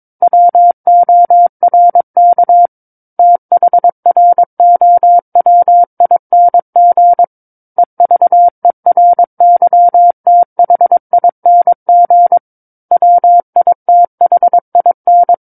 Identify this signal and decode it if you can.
Morse code, heard as a keyed tone, about 22 wpm, 700 Hz. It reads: WORK THROWING EVERYTHING WITHIN
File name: 06-Chapter-06 22 WPM